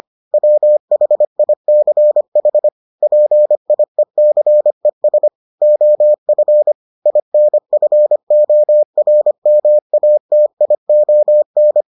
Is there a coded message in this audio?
Yes: Morse code from 0.3 to 11.8 s, heard as a keyed tone, about 25 wpm, 600 Hz. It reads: WHICH PIECES OF INFORMATION